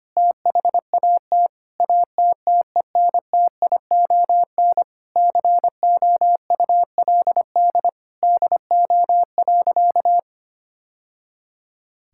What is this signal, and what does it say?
Morse code, heard as a keyed tone, about 25 wpm, 700 Hz. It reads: THAT ATTENTION COULD DO.
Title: Sentences from Top 1000 Words - Review Part 6 - 25wpm